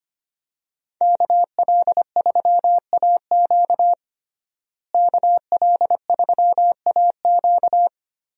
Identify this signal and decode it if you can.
Morse code, heard as a keyed tone, about 25 wpm, 700 Hz. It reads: KL3AQ KL3AQ
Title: USA Call Signs - 5 Characters - Repeated 3x - 25wpm